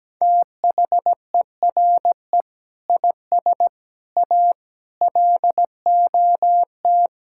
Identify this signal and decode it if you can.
Morse code, heard as a keyed tone, about 17 wpm, 700 Hz. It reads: THERE IS A LOT
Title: Sentences from Top 1000 Words - Review Part 5 - 17wpm